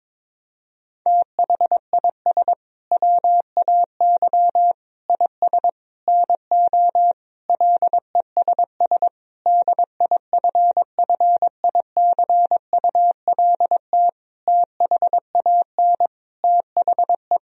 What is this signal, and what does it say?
Morse code, heard as a keyed tone, about 22 wpm, 700 Hz. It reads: THIS WAY IS NO LESS DIFFICULT THAN THE
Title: Sentences from Top 1000 Words - Review Part 2 - 22wpm